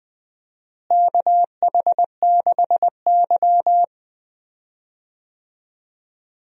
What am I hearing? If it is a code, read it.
Morse code, heard as a keyed tone, about 20 wpm, 700 Hz. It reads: KH6Y
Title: DXCC-100 Call Signs - 4 Characters - Repeated 2x - 20wpm